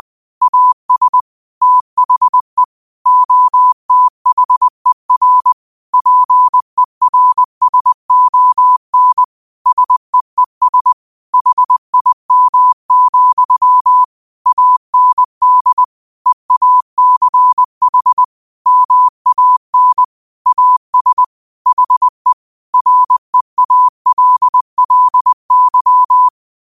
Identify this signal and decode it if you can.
Morse code, heard as a keyed tone, about 20 wpm, 1000 Hz. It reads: AS THE OTHER PERSON SEES HIM, AND EACH MAN AS HE REALLY
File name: qotdfiles / 20wpm